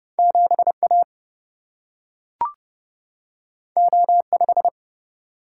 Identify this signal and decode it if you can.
Morse code, heard as a keyed tone, about 30 wpm, 700 Hz. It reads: 7A O5
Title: Single-Letter Single-Number 30wpm